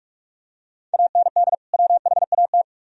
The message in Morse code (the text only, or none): AND WHAT